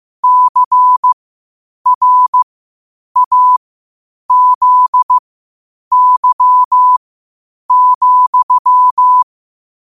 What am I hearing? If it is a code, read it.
Morse code, heard as a keyed tone, about 15 wpm, spaced as for 8 wpm, 1000 Hz. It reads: CRAZY,
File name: qotdfiles / 8wpm